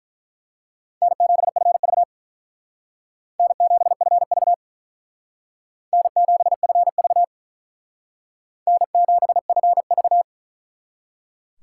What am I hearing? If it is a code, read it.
Morse code, heard as a keyed tone, about 43 wpm, 700 Hz. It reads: N7FV N7FV N7FV N7FV
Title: Speed Racing - USA 1x2 Call Signs 35wpm